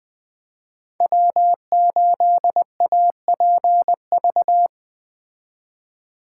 Morse code, heard as a keyed tone, about 21 wpm, 700 Hz. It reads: W8APV
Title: Speed Racing - USA 1x3 Call Signs 17wpm